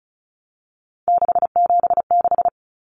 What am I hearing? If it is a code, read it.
Morse code, heard as a keyed tone, about 34 wpm, 700 Hz. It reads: 676
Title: Numbers - 3-Digit 35wpm